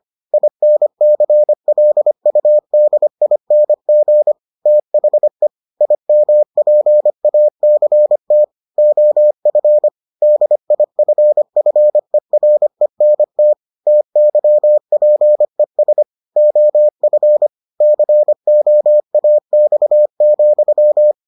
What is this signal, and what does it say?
Morse code, heard as a keyed tone, about 25 wpm, 600 Hz. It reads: INCLUDING THE IMPACT OF DIFFERENT TYPES OF COAX,